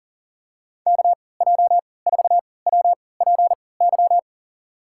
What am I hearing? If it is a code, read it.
Morse code, heard as a keyed tone, about 38 wpm, spaced as for 22 wpm, 700 Hz. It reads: KJ4WPY
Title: USA Call Signs - 6 Characters - 3x Character Spacing - 40wpm